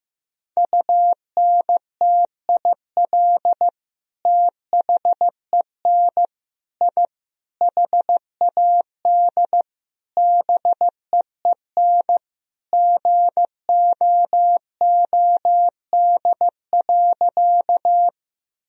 Morse code, heard as a keyed tone, about 15 wpm, 700 Hz. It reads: UNTIL THEN I HAD BEEN GOOD.
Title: Sentences from Top 1000 Words - Review Part 1 - 15wpm